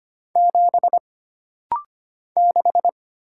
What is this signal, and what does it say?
Morse code, heard as a keyed tone, about 25 wpm, 700 Hz. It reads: 7 6
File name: Single Character (Letter:Number:Punctuation:Prosign) - Rapid-Fire - 25wpm